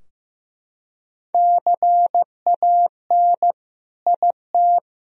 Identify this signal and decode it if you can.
Morse code, heard as a keyed tone, about 16 wpm, 700 Hz. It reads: CAN IT
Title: Sentences from Top 1000 Words - Review Part 2 - 15wpm